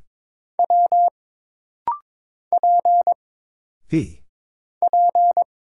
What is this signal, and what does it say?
Morse code, heard as a keyed tone, about 22 wpm, 700 Hz. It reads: W P P
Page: This rapid-fire timing is geared to push your ICR (Instant Character Recognition) skills.